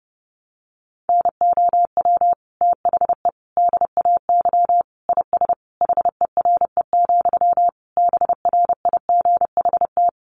Morse code, heard as a keyed tone, about 30 wpm, 700 Hz. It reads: NOW THE DAY IS HERE, BRIGHT
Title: Sentences from Top 1000 Words 30wpm